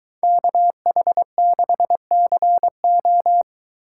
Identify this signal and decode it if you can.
Morse code, heard as a keyed tone, about 21 wpm, 700 Hz. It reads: KH6CO